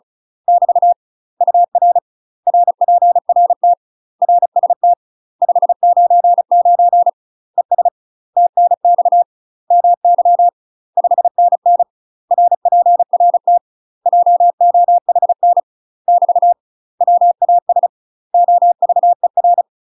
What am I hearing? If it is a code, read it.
Morse code, heard as a keyed tone, about 36 wpm, 700 Hz. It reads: <BT> UR RPRT RST 599 ES TNX MY 5NN RPRT JOHN <BT> WAS OVER
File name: UK MASTER QSOs TRAINING / 35WPM / UK MASTER QSOs 0000